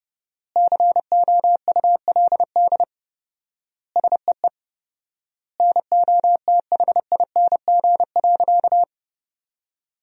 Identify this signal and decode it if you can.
Morse code, heard as a keyed tone, about 30 wpm, 700 Hz. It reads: COULD SEE NOTHING.
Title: 4x Word-Spacing - Sentences from Top 500 Words - 30wpm